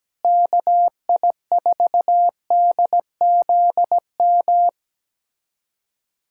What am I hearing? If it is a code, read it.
Morse code, heard as a keyed tone, about 17 wpm, 700 Hz. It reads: KI4DZM
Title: USA Callsign DE Callsign 17wpm